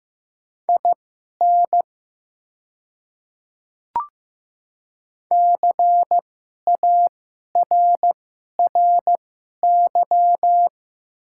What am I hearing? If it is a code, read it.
Morse code, heard as a keyed tone, about 15 wpm, spaced as for 11 wpm, 700 Hz. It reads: IN CARRY
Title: 2x Character Spacing - Top 300 Words - 15wpm